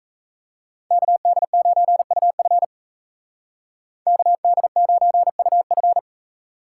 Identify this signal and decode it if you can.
Morse code, heard as a keyed tone, about 33 wpm, 700 Hz. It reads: KD9UF KD9UF
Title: Speed Racing - USA 2x2 Call Signs 28wpm